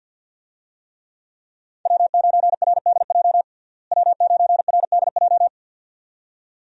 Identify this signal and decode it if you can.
Morse code, heard as a keyed tone, about 50 wpm, 700 Hz. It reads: W9RDJ W9RDJ
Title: USA Call Signs - 5 Characters - Repeated 2x - 50wpm